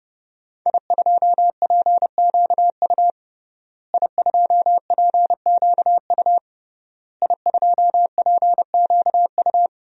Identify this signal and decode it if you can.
Morse code, heard as a keyed tone, about 30 wpm, 700 Hz. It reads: I2PQU I2PQU I2PQU
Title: DXCC-100 Call Signs - 5 Characters - Repeated 3x - 30wpm